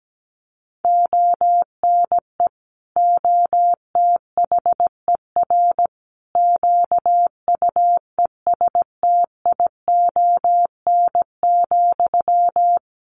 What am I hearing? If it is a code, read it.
Morse code, heard as a keyed tone, about 17 wpm, 700 Hz. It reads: ONE OTHER QUESTION,
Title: Sentences from Top 400 Words 17wpm